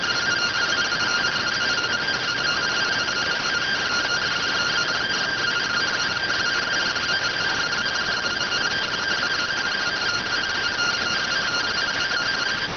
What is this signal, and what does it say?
Two-tone teleprinter signal, about 170 Hz shift.